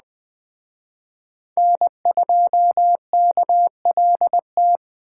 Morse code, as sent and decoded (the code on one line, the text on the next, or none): -. ..--- -.- .-.. -
N2KLT